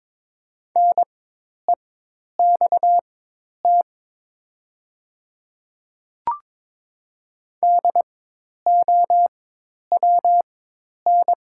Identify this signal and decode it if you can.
Morse code, heard as a keyed tone, about 22 wpm, spaced as for 10 wpm, 690 Hz. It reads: NEXT DOWN
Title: 4x Character Spacing - Top 300 Words - 22wpm